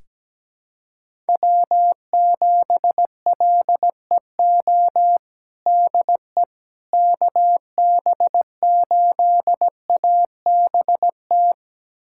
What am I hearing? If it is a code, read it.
Morse code, heard as a keyed tone, about 17 wpm, 700 Hz. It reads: W7LEO DE KB8ABT
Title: USA Callsign DE Callsign 17wpm